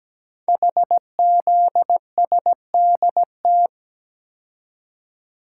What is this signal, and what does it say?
Morse code, heard as a keyed tone, about 17 wpm, 700 Hz. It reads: HZSDT
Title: Code Groups - 5 Letters - 17wpm